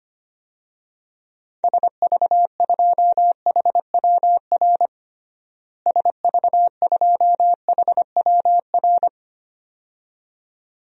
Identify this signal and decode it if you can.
Morse code, heard as a keyed tone, about 25 wpm, 700 Hz. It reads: SV2HWR SV2HWR
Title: DXCC-100 Call Signs - 6 Characters - Repeated 2x - 25wpm